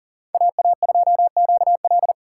Morse code, heard as a keyed tone, about 39 wpm, 700 Hz. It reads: AA2QL
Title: USA Call Signs - 5 Characters - 2x Character Spacing - 40wpm